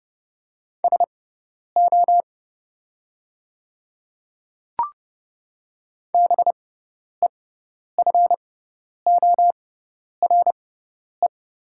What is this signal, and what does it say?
Morse code, heard as a keyed tone, about 30 wpm, spaced as for 10 wpm, 700 Hz. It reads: SO BEFORE
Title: 6x Character Spacing - Top 300 Words - 30wpm